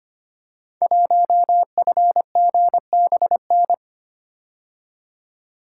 Morse code, heard as a keyed tone, about 25 wpm, 700 Hz. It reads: 1FGBN
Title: Code Groups - 5 Letter:Numbers - 25wpm